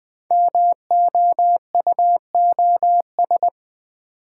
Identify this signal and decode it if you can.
Morse code, heard as a keyed tone, about 20 wpm, 700 Hz. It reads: MOUOS
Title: Code Groups - 5 Letters - 20wpm